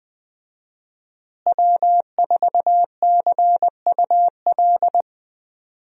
Morse code, heard as a keyed tone, about 20 wpm, 700 Hz. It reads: W4CUL
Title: USA Call Signs - 5 Characters - Repeated 2x - 20wpm